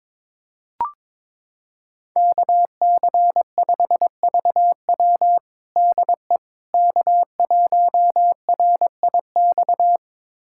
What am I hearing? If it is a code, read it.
Morse code, heard as a keyed tone, about 22 wpm, 700 Hz. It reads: KC5VW DE K1RIX
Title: USA Callsign DE Callsign 22wpm